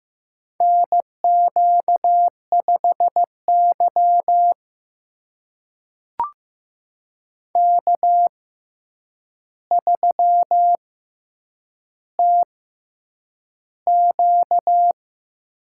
Morse code, heard as a keyed tone, about 16 wpm, spaced as for 8 wpm, 700 Hz. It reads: NQ5Y K3TQ
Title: USA Call Signs - 6x Character Spacing - 15wpm